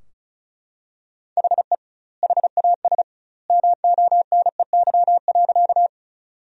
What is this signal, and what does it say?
Morse code, heard as a keyed tone, about 36 wpm, 700 Hz. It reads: HE HAS MONEY.
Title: 2x Word-Spacing - Sentences from Top 400 Words - 35wpm